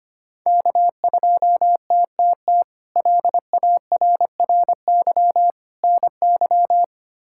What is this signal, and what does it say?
Morse code, heard as a keyed tone, about 25 wpm, 700 Hz. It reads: K2TTT LARRY NY
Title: K1USN SST QSO Practice - 25wpm